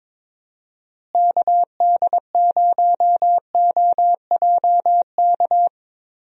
Morse code, heard as a keyed tone, about 22 wpm, 700 Hz. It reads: KD0OJK
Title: USA Call Signs - 6 Characters - Repeated 3x - 22wpm